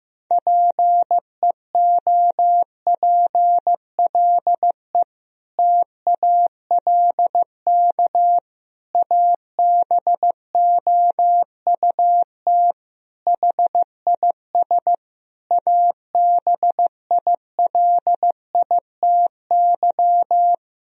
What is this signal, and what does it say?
Morse code, heard as a keyed tone, about 15 wpm, 700 Hz. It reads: PEOPLE TALK ABOUT HIS ABILITY
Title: Sentences from Top 1000 Words - Review Part 3 - 15wpm